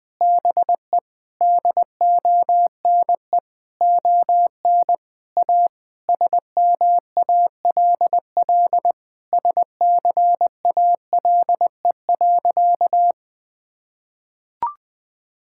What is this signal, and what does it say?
Morse code, heard as a keyed tone, about 21 wpm, 700 Hz. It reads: BE DONE ON A SMALL SCALE.
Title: Sentences from Top 1000 Words - Review Part 5 - 20wpm